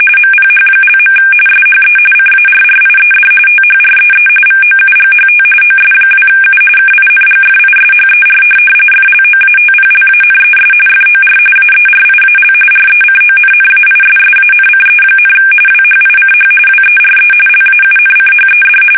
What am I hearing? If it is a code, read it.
RTTY, about 45.45 baud. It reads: MPH). SHE CARRIED THREE VOUGHT OS2U KINGFISHER FLOATPLANES FOR AERIAL RECONNAISSANCE, WHICH WERE LAUNCHED
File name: fsk850 / USS-Massachusetts